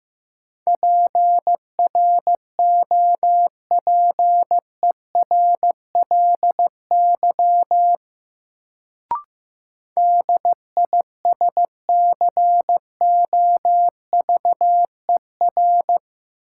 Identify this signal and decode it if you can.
Morse code, heard as a keyed tone, about 15 wpm, 700 Hz. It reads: PROPERLY DISCOVER
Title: Sentences from Top 1100 Words - New Word Drill - 15wpm